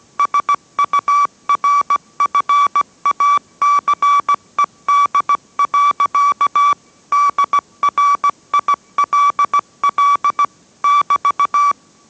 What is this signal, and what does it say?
Morse code, heard as a keyed tone, about 17 wpm, 1150 Hz. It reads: SURFACED. DRILL <BT>